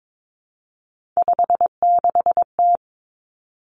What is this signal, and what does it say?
Morse code, heard as a keyed tone, about 22 wpm, 700 Hz. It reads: 56T
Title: Three Challenging Letter-Number Combinations 22wpm